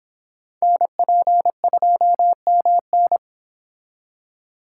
Morse code, heard as a keyed tone, about 29 wpm, 700 Hz. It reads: NP2MN